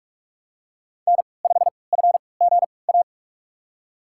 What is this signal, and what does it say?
Morse code, heard as a keyed tone, about 47 wpm, spaced as for 26 wpm, 700 Hz. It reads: N5FGA
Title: USA Call Signs - 5 Characters - 3x Character Spacing - 45wpm